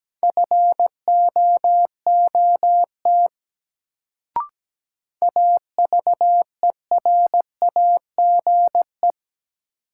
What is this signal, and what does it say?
Morse code, heard as a keyed tone, about 17 wpm, 700 Hz. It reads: FOOT AVERAGE
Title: Sets of 5 Words - New Word Drill - 17wpm